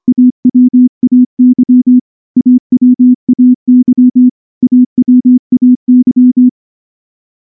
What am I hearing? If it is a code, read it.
Morse code, heard as a keyed tone, about 24 wpm, 250 Hz. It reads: AWAY AWAY AWAY
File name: morse / morse-training-25WPM